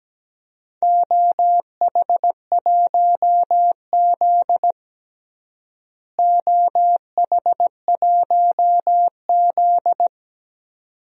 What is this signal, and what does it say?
Morse code, heard as a keyed tone, about 17 wpm, 700 Hz. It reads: OH1Z OH1Z
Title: DXCC-100 Call Signs - 4 Characters - Repeated 2x - 17wpm